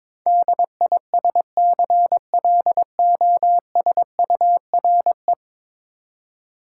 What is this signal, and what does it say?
Morse code, heard as a keyed tone, about 22 wpm, 700 Hz. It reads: DISCLOSURE
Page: morse-code-ninja-practice-22wpm